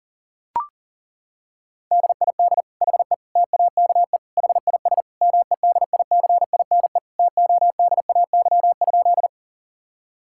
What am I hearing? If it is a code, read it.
Morse code, heard as a keyed tone, about 40 wpm, 700 Hz. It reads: DID HE TAKE HIS MEDICINE TODAY?
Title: Sentences from Top 1000 Words - Review Part 6 - 40wpm